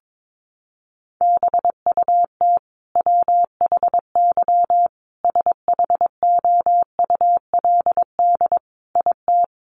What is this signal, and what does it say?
Morse code, heard as a keyed tone, about 22 wpm, 700 Hz. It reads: BUT WHY SHOULD IT
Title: Sentences from Top 1200 Words 22wpm